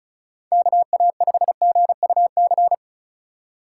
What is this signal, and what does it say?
Morse code, heard as a keyed tone, about 43 wpm, 700 Hz. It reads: KA5GUC
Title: Speed Racing - USA 2x3 Call Signs 35wpm